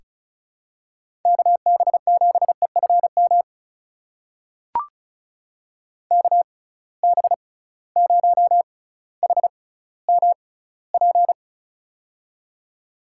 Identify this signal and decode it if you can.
Morse code, heard as a keyed tone, about 36 wpm, spaced as for 18 wpm, 690 Hz. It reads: KB7EFM KB0HMP
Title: USA Call Signs - 6 Characters - 6x Character Spacing - 35wpm